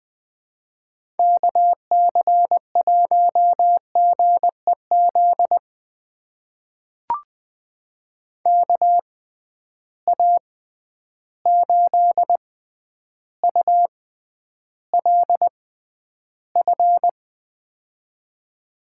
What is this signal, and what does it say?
Morse code, heard as a keyed tone, about 20 wpm, spaced as for 10 wpm, 700 Hz. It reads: KC1GEZ KA8ULF
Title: USA Call Signs - 6 Characters - 6x Character Spacing - 20wpm